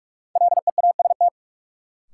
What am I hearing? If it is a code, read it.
Morse code, heard as a keyed tone, about 47 wpm, 700 Hz. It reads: LEAST
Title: Sets of 2 Words - New Word Drill - 45wpm